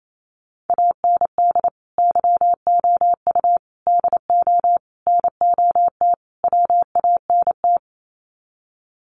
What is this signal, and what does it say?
Morse code, heard as a keyed tone, about 28 wpm, 700 Hz. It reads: AND YOU DO NOT WANT
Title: Sets of 5 Words Encore2 28wpm